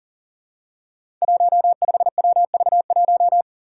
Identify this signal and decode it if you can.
Morse code, heard as a keyed tone, about 40 wpm, 700 Hz. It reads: 15WV1
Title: Five Challenging Letter-Number Combinations 40wpm